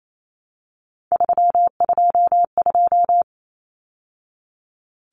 Morse code, heard as a keyed tone, about 28 wpm, 700 Hz. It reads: 322
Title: Numbers - 3-Digit 28wpm